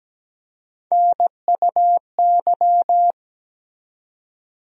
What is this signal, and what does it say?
Morse code, heard as a keyed tone, about 17 wpm, 700 Hz. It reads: NUY
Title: USA Call Sign Postfixes - 3 Letters 17wpm